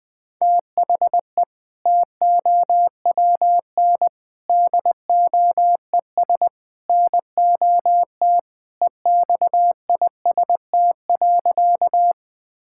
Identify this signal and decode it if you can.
Morse code, heard as a keyed tone, about 20 wpm, 700 Hz. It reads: THE TOWN DOES NOT EXIST.
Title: Sentences from Top 1000 Words - Review Part 5 - 20wpm